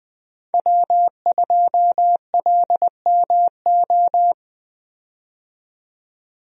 Morse code, heard as a keyed tone, about 20 wpm, 700 Hz. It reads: W2LMO
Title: USA Call Signs - 5 Characters - 20wpm